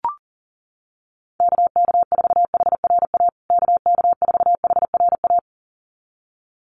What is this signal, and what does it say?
Morse code, heard as a keyed tone, about 40 wpm, 700 Hz. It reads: KK4HRA KK4HRA
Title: USA Call Signs Repeated 2x 40wpm